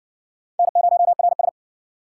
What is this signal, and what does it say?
Morse code, heard as a keyed tone, about 51 wpm, 700 Hz. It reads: N0RS